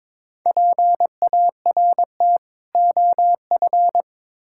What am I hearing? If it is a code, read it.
Morse code, heard as a keyed tone, about 22 wpm, 700 Hz. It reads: PART OF
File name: Build A Phrase - 2-Word Phrases - 22wpm